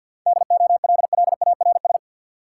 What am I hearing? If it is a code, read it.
Morse code, heard as a keyed tone, about 49 wpm, 700 Hz. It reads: DOLLARS
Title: Sets of 5 Words - New Word Drill - 50wpm